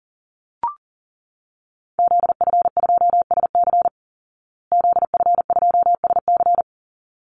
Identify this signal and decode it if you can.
Morse code, heard as a keyed tone, about 40 wpm, 700 Hz. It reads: ZF2SC ZF2SC
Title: Call Signs Repeated 3x - 40wpm